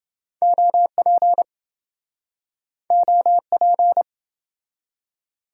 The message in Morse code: OP OP

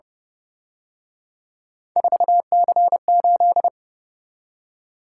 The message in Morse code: 4C8

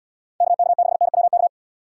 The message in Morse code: KK6NWK

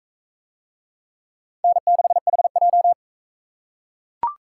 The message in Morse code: N6HJ

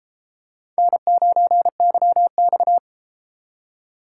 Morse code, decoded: N9YX

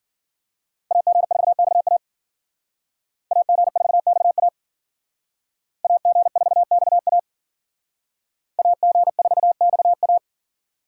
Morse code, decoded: AG4XA AG4XA AG4XA AG4XA

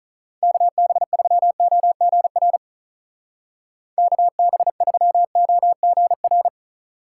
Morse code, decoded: KB3OGR KB3OGR

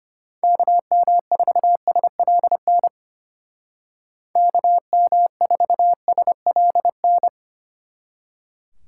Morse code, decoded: KM4SLN KM4SLN

